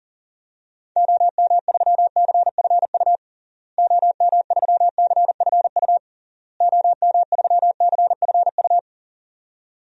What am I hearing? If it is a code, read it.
Morse code, heard as a keyed tone, about 40 wpm, 700 Hz. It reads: OM3CFU OM3CFU OM3CFU
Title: DXCC-100 Call Signs - 6 Characters - Repeated 3x - 40wpm